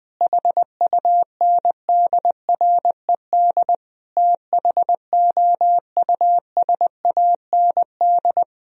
Morse code, heard as a keyed tone, about 20 wpm, 700 Hz. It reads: HUNDRED THOUSAND
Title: Sentences from Top 1000 Words - Review Part 3 - 20wpm